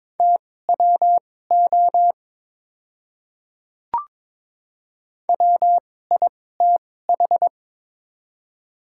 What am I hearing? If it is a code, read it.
Morse code, heard as a keyed tone, about 22 wpm, spaced as for 16 wpm, 700 Hz. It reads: TWO WITH
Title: 2x Character Spacing - Top 100 Words - 22wpm